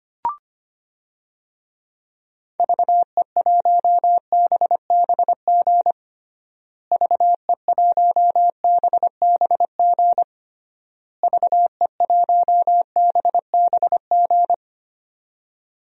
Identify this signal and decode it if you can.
Morse code, heard as a keyed tone, about 25 wpm, 700 Hz. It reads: VE1BBG VE1BBG VE1BBG
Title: DXCC-100 Call Signs - 6 Characters - Repeated 3x - 25wpm